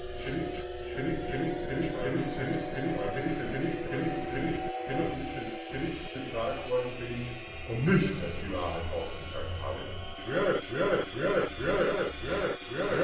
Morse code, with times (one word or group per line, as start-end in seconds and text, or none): none